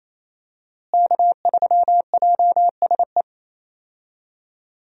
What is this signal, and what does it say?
Morse code, heard as a keyed tone, about 28 wpm, 700 Hz. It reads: K3JSE
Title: USA Call Signs - 5 Characters - Repeated 3x - 28wpm